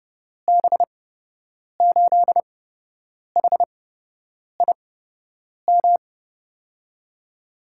Morse code, heard as a keyed tone, about 32 wpm, spaced as for 8 wpm, 700 Hz. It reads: B8HIM
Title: USA Call Signs - 6 Characters - 8x Character Spacing - 30wpm